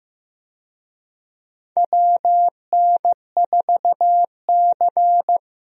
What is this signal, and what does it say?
Morse code, heard as a keyed tone, about 15 wpm, 700 Hz. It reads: WN4C
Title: USA Call Signs - 4 Characters - Repeated 3x - 15wpm